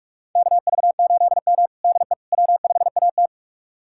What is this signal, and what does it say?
Morse code, heard as a keyed tone, about 41 wpm, 700 Hz. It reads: KV8M DE W5AT
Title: USA Callsign DE Callsign - 4 Characters - 45wpm